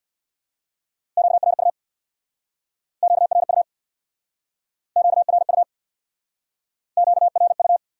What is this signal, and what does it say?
Morse code, heard as a keyed tone, about 70 wpm, 700 Hz. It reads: QRU QRU QRU QRU